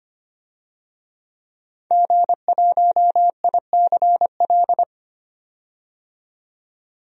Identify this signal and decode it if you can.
Morse code, heard as a keyed tone, about 25 wpm, 700 Hz. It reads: G1ICL
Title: DXCC-100 Call Signs - 5 Characters - 25wpm